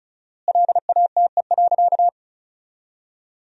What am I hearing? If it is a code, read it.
Morse code, heard as a keyed tone, about 36 wpm, 700 Hz. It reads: LATE.
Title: 4x Word-Spacing - Sentences from Top 300 Words - 35wpm